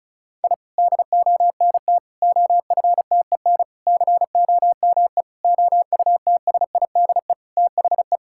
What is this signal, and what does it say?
Morse code, heard as a keyed tone, about 36 wpm, 700 Hz. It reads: I DONT OFTEN COME OUTSIDE THE
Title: Sentences from Top 1000 Words - Review Part 6 - 35wpm